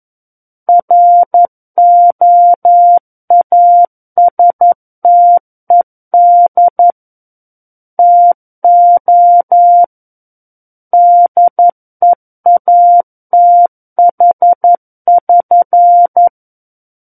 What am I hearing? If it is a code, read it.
Morse code, heard as a keyed tone, about 11 wpm, 700 Hz. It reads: ROASTED TO DEATH<SN>
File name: 11 WPM